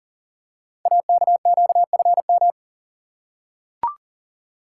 Morse code, heard as a keyed tone, about 38 wpm, 690 Hz. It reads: AKQFM
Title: Code Groups - 5 Letter:Numbers - 40wpm